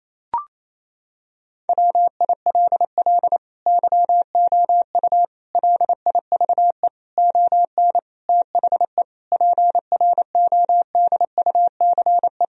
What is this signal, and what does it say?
Morse code, heard as a keyed tone, about 27 wpm, 700 Hz. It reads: WILL YOU LIVE ON THE PRODUCE
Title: Sentences from Top 1000 Words - Review Part 3 - 28wpm